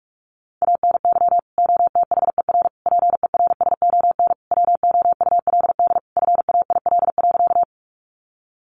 Morse code, heard as a keyed tone, about 44 wpm, 700 Hz. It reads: ANY OTHER PERSON WOULD FAIL.
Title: Sentences from Top 1100 Words 45wpm